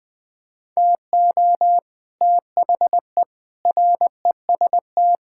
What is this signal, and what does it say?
Morse code, heard as a keyed tone, about 20 wpm, 700 Hz. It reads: TO THE REST
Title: Sets of 6 Words - 20wpm